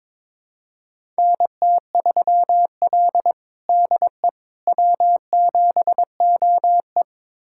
Morse code, heard as a keyed tone, about 22 wpm, 700 Hz. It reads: NT3L DE W7OE
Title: USA Callsign DE Callsign - 4 Characters - 22wpm